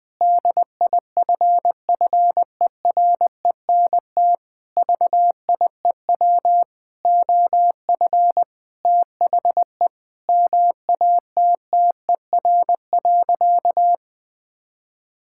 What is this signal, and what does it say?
Morse code, heard as a keyed tone, about 21 wpm, 700 Hz. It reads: DIFFERENT VIEW OF THE MATTER.
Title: Sentences from Top 1600 Words 20wpm